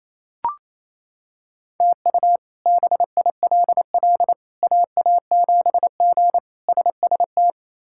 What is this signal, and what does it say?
Morse code, heard as a keyed tone, about 27 wpm, 700 Hz. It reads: TU BILL AA7G SST
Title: K1USN SST QSO Practice - 28wpm